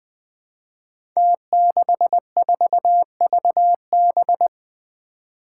Morse code, heard as a keyed tone, about 20 wpm, 700 Hz. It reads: T64VB
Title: Five Challenging Letter-Number Combinations 20wpm